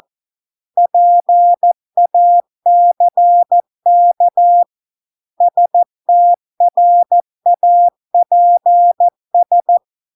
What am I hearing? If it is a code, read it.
Morse code, heard as a keyed tone, about 14 wpm, 700 Hz. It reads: PACK STRAPS